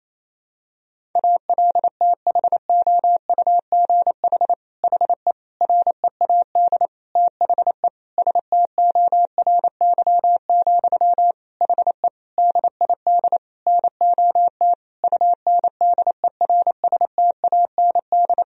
Morse code, heard as a keyed tone, about 28 wpm, 700 Hz. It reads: ALTHOUGH HE READ THE STORY, HE DID NOT UNDERSTAND
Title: Sentences from Top 1000 Words - Review Part 3 - 28wpm